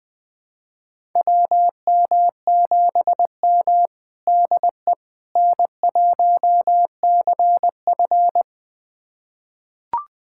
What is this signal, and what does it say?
Morse code, heard as a keyed tone, about 20 wpm, 700 Hz. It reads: WM7M DE N1CF
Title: USA Callsign DE Callsign - 4 Characters - 20wpm